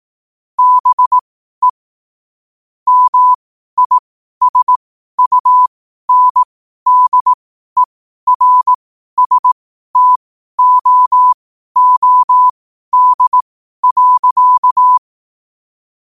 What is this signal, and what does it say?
Morse code, heard as a keyed tone, about 18 wpm, spaced as for 12 wpm, 1000 Hz. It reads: BE MISUNDERSTOOD.
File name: 12wpm